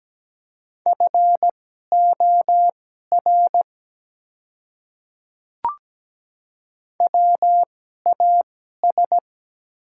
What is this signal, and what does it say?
Morse code, heard as a keyed tone, about 17 wpm, spaced as for 12 wpm, 700 Hz. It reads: FOR WAS
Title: 2x Character Spacing - Top 100 Words - 17wpm